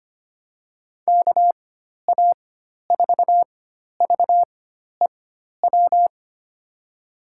Morse code, heard as a keyed tone, about 25 wpm, spaced as for 12 wpm, 700 Hz. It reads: KA4VEW
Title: USA Call Signs - 6 Characters - 4x Character Spacing - 25wpm